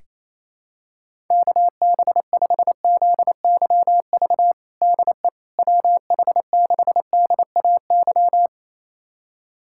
Morse code, heard as a keyed tone, about 28 wpm, 700 Hz. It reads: KB5ZYV DE WH6DAY
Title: USA Callsign DE Callsign - 6 Characters - 28wpm